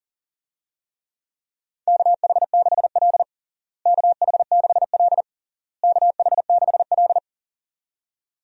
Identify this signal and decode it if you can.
Morse code, heard as a keyed tone, about 37 wpm, 700 Hz. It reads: KH6L KH6L KH6L
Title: DXCC-100 Call Signs - 4 Characters - Repeated 3x - 40wpm